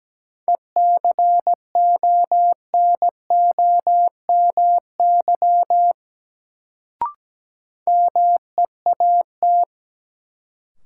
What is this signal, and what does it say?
Morse code, heard as a keyed tone, about 17 wpm, 700 Hz. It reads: ECONOMY MEAT
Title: Sentences from Top 1000 Words - Review Part 3 - New Word Drill - 17wpm